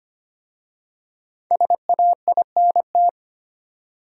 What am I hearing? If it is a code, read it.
Morse code, heard as a keyed tone, about 25 wpm, 700 Hz. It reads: SAINT
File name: 5 Letter Words - First 3 Letters Given - 25wpm